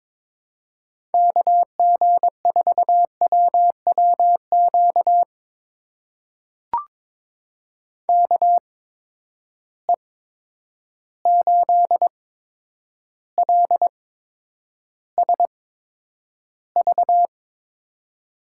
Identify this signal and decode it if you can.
Morse code, heard as a keyed tone, about 22 wpm, spaced as for 9 wpm, 700 Hz. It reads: KG4WWQ KE8LSV
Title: USA Call Signs - 6 Characters - 8x Character Spacing - 22wpm